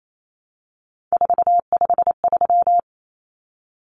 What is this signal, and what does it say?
Morse code, heard as a keyed tone, about 28 wpm, 700 Hz. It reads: 453
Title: Common Cut Numbers - 3-Digit 28wpm